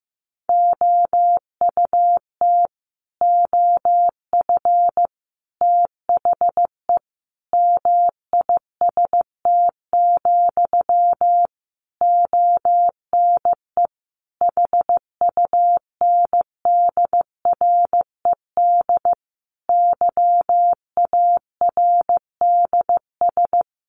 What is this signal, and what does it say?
Morse code, heard as a keyed tone, about 15 wpm, 700 Hz. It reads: OUT OF THE MIST, ONE HUNDRED YARDS
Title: A Sound Of Thunder 15wpm